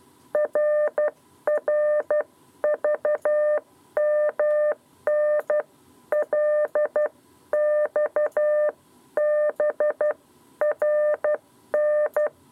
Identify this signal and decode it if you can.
Morse code, heard as a keyed tone, about 12 wpm, spaced as for 10 wpm, 600 Hz. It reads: RRVMN LXBRN
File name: 7b lezione cw 2